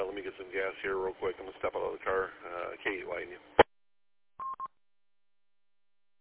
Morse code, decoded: none

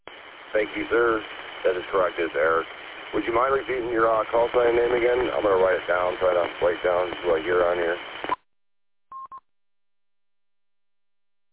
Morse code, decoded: none

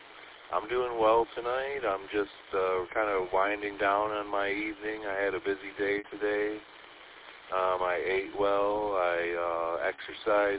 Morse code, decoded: none

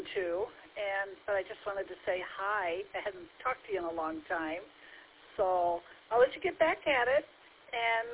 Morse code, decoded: none

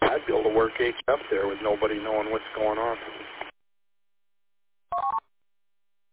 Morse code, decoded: none

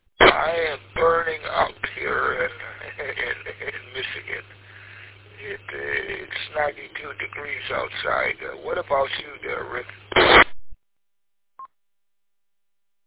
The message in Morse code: none

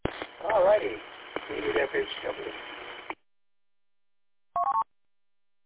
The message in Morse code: none

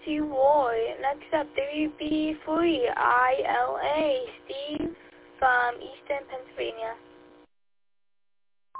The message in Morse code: none

